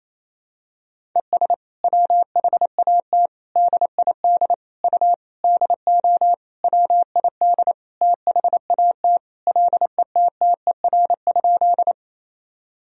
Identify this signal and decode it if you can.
Morse code, heard as a keyed tone, about 28 wpm, 700 Hz. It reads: ES WHAT DID U DO WID THAT LETTER?
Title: Sentences from Top 300 Words and Abbreviations 28wpm